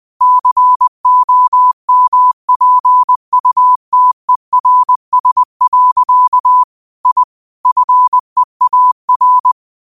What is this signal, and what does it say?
Morse code, heard as a keyed tone, about 20 wpm, 1000 Hz. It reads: COMPUTERS. I FEAR
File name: qotdfiles / 20wpm